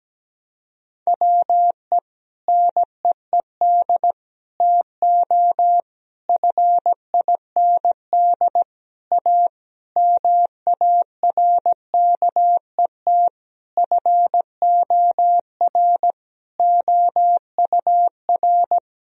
Morse code, heard as a keyed tone, about 17 wpm, 700 Hz. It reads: WE NEED TO FIND A MARKET FOR OUR
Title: Sentences from Top 1000 Words - Review Part 3 - 17wpm